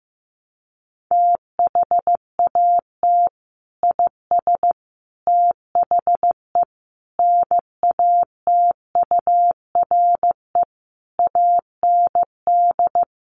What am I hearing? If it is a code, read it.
Morse code, heard as a keyed tone, about 15 wpm, 700 Hz. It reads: THAT IS THE NATURE AND
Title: Sentences from Top 1100 Words 15wpm